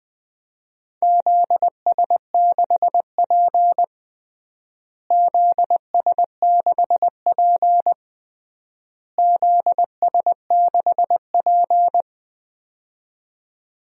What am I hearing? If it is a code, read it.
Morse code, heard as a keyed tone, about 20 wpm, 700 Hz. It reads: ZS6P ZS6P ZS6P
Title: DXCC-100 Call Signs - 4 Characters - Repeated 3x - 20wpm